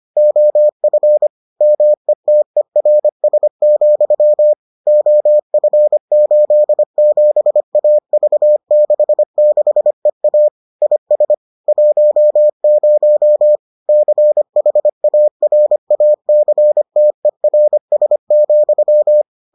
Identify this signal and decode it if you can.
Morse code, heard as a keyed tone, about 25 wpm, 600 Hz. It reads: OF METERS, OF87AV66EA IS 10 CHARACTERS,